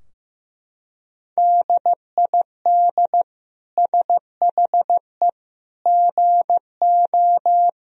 Morse code, heard as a keyed tone, about 15 wpm, 700 Hz. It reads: DID SHE GO
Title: Sentences from Top 1000 Words - Review Part 3 - 15wpm